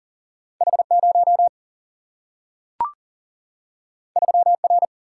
Morse code, heard as a keyed tone, about 39 wpm, 700 Hz. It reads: H0 3R
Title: Single-Letter Single-Number 40wpm